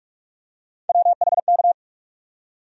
Morse code, heard as a keyed tone, about 46 wpm, 700 Hz. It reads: WHK
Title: USA Call Sign Postfixes - 3 Letters 45wpm